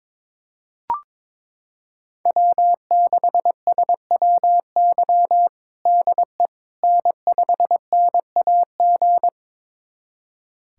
Morse code, heard as a keyed tone, about 21 wpm, 700 Hz. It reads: W6SWY DE N5NAG
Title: USA Callsign DE Callsign 22wpm